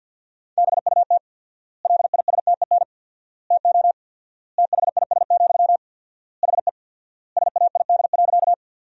Morse code, heard as a keyed tone, about 50 wpm, 700 Hz. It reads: BUT LISTEN TO THIS, HE SAID.
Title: 4x Word-Spacing - Sentences from Top 400 Words - 50wpm